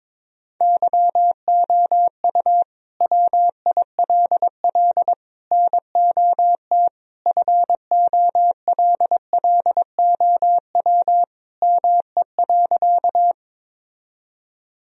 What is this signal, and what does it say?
Morse code, heard as a keyed tone, about 22 wpm, 700 Hz. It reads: YOU WILL NOT FOLLOW ME.
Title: Sentences from Top 1000 Words - Review Part 1 - 22wpm